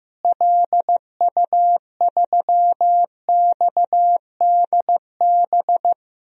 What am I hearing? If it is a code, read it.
Morse code, heard as a keyed tone, about 15 wpm, 700 Hz. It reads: LU3XDB
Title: DXCC-100 Call Signs - 6 Characters - Repeated 3x - 15wpm